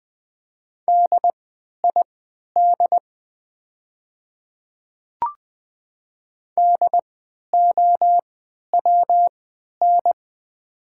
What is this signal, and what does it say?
Morse code, heard as a keyed tone, about 20 wpm, spaced as for 11 wpm, 700 Hz. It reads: DID DOWN
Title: 3x Character Spacing - Top 300 Words - 20wpm